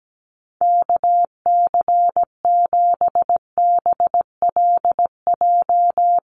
Morse code, heard as a keyed tone, about 17 wpm, 700 Hz. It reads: KC7BLJ
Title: USA Call Signs 17wpm